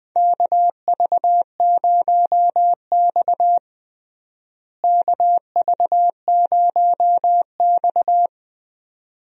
Morse code, heard as a keyed tone, about 21 wpm, 700 Hz. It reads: KV0X KV0X
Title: USA Call Signs - 4 Characters - Repeated 2x - 20wpm